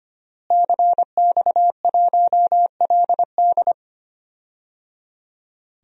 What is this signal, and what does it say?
Morse code, heard as a keyed tone, about 25 wpm, 700 Hz. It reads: CX1LD
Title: DXCC-100 Call Signs - 5 Characters - Repeated 3x - 25wpm